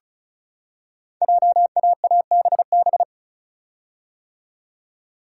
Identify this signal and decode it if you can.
Morse code, heard as a keyed tone, about 36 wpm, 690 Hz. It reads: JAABB
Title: Code Groups - 5 Letters - 35wpm